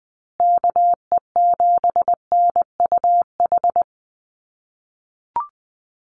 Morse code, heard as a keyed tone, about 20 wpm, 700 Hz. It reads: KE7NUH
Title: USA Call Signs Repeated 3x 20wpm